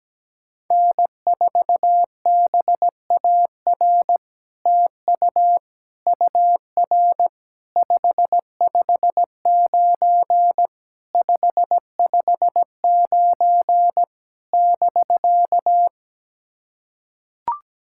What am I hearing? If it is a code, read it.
Morse code, heard as a keyed tone, about 17 wpm, 700 Hz. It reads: N4BAR TU UR 559 559 <BK>
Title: SOTA QSO Practice 17wpm